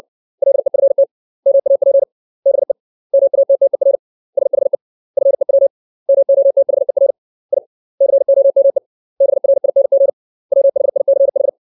Morse code, heard as a keyed tone, about 60 wpm, 550 Hz. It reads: LEFT MAP BE MATTER SHE FEW MOTHER I COME BRING WHICH
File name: 632 words / 632 Words 60WPM